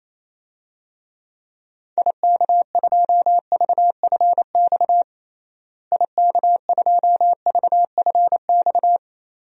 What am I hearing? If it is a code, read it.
Morse code, heard as a keyed tone, about 28 wpm, 700 Hz. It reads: IK2VFX IK2VFX
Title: DXCC-100 Call Signs - 6 Characters - Repeated 2x - 28wpm